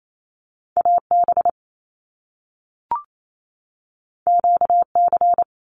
Morse code, heard as a keyed tone, about 28 wpm, 700 Hz. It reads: AB QC
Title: Canadian Provinces and Territories 28wpm